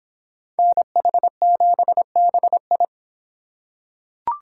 Morse code, heard as a keyed tone, about 26 wpm, 700 Hz. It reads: NH7BI